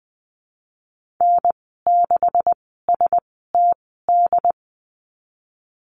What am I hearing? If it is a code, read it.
Morse code, heard as a keyed tone, about 20 wpm, spaced as for 14 wpm, 700 Hz. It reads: N6STD
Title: 167 Lesson 27g - Number 7 - Call Signs 20wpm